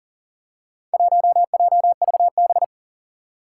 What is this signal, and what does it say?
Morse code, heard as a keyed tone, about 40 wpm, 700 Hz. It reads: 1JVB
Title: Four Challenging Letter-Number Combinations 40wpm